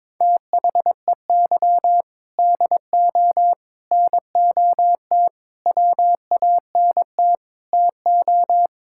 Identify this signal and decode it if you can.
Morse code, heard as a keyed tone, about 21 wpm, 700 Hz. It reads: THEY DO NOT WANT TO
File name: practice / 22wpm / Sentences with 4-Word Phrases - 22wpm